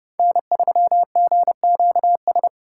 Morse code, heard as a keyed tone, about 30 wpm, 700 Hz. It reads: N3GQS
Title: USA Call Signs - 5 Characters - 2x Character Spacing - 30wpm